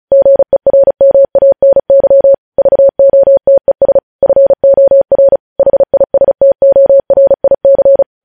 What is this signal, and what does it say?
Morse code, heard as a keyed tone, about 36 wpm, 550 Hz. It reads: GERMANY VOTES FOR HISTORIC
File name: podcast / News headlines-1x-35x35